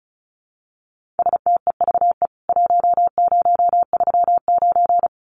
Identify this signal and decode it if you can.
Morse code, heard as a keyed tone, about 34 wpm, 700 Hz. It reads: STEVE 1039
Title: CWT Exchange 35wpm